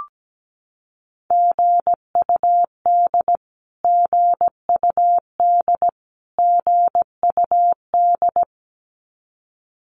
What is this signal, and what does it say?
Morse code, heard as a keyed tone, about 17 wpm, 700 Hz. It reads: GUD GUD GUD
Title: Abbreviations Repeated 3x - 17wpm